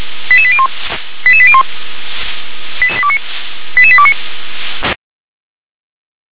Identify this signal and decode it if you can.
Five-tone selective call (ZVEI), heard as quick runs of five tones, tones of about 70 ms. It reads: 88801 88801 88829
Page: ZVEI